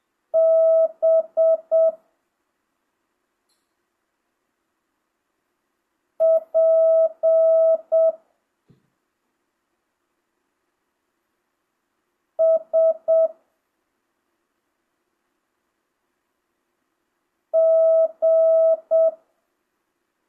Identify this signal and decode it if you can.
Morse code, heard as a keyed tone, about 7 wpm, 650 Hz. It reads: B P S G